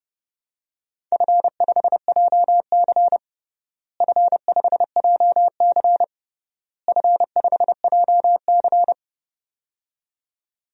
Morse code, heard as a keyed tone, about 30 wpm, 700 Hz. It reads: F5JC F5JC F5JC
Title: DXCC-100 Call Signs - 4 Characters - Repeated 3x - 30wpm